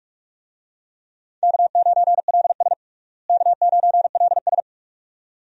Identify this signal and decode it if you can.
Morse code, heard as a keyed tone, about 46 wpm, 700 Hz. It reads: K9LS K9LS
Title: USA Call Signs - 4 Characters - Repeated 2x - 45wpm